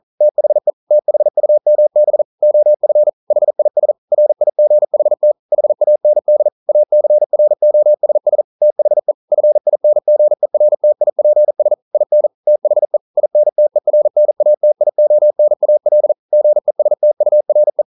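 Morse code, heard as a keyed tone, about 41 wpm, 600 Hz. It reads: THE THUMB OF HIS RIGHT HAND ACROSS THE FINGERTIPS IN THE INTERNATIONAL GESTURE
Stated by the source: Morse code at 40 wpm, practice text, CW med Gnister.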